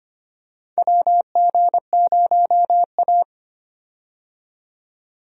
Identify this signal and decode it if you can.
Morse code, heard as a keyed tone, about 25 wpm, 700 Hz. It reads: WG0A
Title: USA Call Signs - 4 Characters - 25wpm